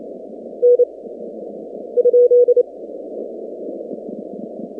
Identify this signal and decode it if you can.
Morse code, heard as a keyed tone, about 27 wpm, 490 Hz. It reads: N ?
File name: 2019-cw-wpx-kt5j-k5pi